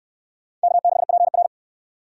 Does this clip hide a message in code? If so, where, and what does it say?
Yes: Morse code from 0.6 to 1.5 s, heard as a keyed tone, about 66 wpm, 700 Hz. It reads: K6PR